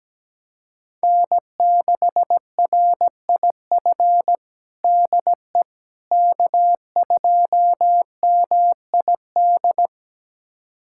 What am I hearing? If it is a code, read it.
Morse code, heard as a keyed tone, about 17 wpm, 700 Hz. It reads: N6RIF DE K2MID
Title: USA Callsign DE Callsign - 5 Characters - 17wpm